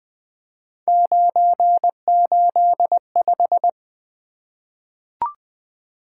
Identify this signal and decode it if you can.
Morse code, heard as a keyed tone, about 20 wpm, 700 Hz. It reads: 985
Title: Three Letter-Number Combinations - 20wpm